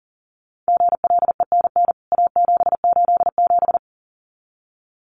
Morse code, heard as a keyed tone, about 40 wpm, 700 Hz. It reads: GLENN A787
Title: CWT Exchange 40wpm